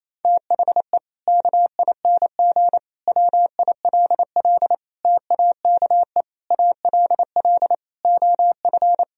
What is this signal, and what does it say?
Morse code, heard as a keyed tone, about 28 wpm, 700 Hz. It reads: THE KING WILL TAKE ALL OF
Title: Sentences from Top 1000 Words - Review Part 4 - 28wpm